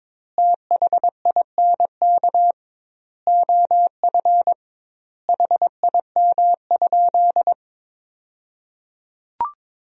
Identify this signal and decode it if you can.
Morse code, heard as a keyed tone, about 22 wpm, 700 Hz. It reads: THINK OF HIM?
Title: 2x Word-Spacing - Sentences from Top 200 Words - 22wpm